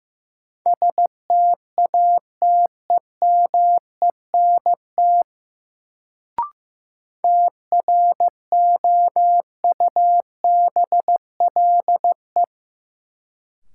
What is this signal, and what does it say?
Morse code, heard as a keyed tone, about 15 wpm, 700 Hz. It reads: STATEMENT TROUBLE
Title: Sentences from Top 1200 Words - New Word Drill - 15wpm